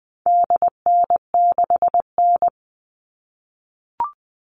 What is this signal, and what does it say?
Morse code, heard as a keyed tone, about 19 wpm, 700 Hz. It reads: DN6N
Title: All Cut Numbers - 4-Digit 20wpm